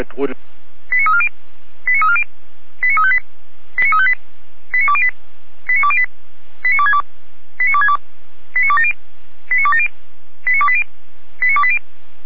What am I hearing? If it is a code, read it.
Five-tone selective call (ZVEI), heard as quick runs of five tones, tones of about 70 ms, with speech. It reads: 89340 89340 89358 89358 89208 89208 89262 89262 89270 89270 89280 89280